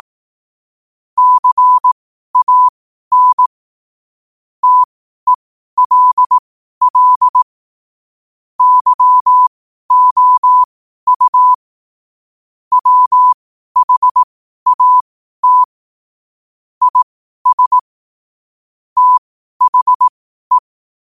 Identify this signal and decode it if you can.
Morse code, heard as a keyed tone, about 18 wpm, spaced as for 12 wpm, 1000 Hz. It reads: CAN TELL YOU WHAT IS THE
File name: qotdfiles / 12wpm